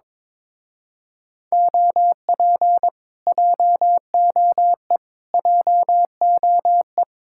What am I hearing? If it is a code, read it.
Morse code, heard as a keyed tone, about 22 wpm, 700 Hz. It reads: OP JOE JOE
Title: Speed-Racing Common QSO Phrases - 22wpm